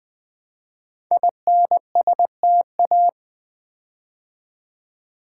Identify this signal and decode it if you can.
Morse code, heard as a keyed tone, about 20 wpm, 700 Hz. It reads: INSTA
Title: Build A Word - 5 Letters At A Time - 12-Letter Words - 20wpm